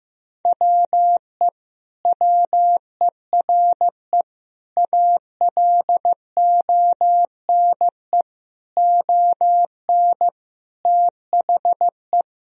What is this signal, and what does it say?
Morse code, heard as a keyed tone, about 15 wpm, 700 Hz. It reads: WE WERE ALONE ON THE
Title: Sentences from Top 1000 Words - Review Part 3 - 15wpm